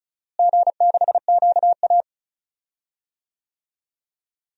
Morse code, heard as a keyed tone, about 36 wpm, 700 Hz. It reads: G6QA
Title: DXCC-100 Call Signs - 4 Characters - Repeated 2x - 35wpm